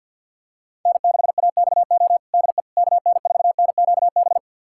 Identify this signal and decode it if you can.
Morse code, heard as a keyed tone, about 47 wpm, 700 Hz. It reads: N6AXO DE KN4NQB
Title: USA Callsign DE Callsign 50wpm